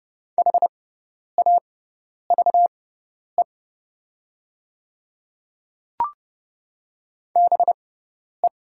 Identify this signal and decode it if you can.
Morse code, heard as a keyed tone, about 30 wpm, spaced as for 10 wpm, 700 Hz. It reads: HAVE BE
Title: 6x Character Spacing - Top 100 Words - 30wpm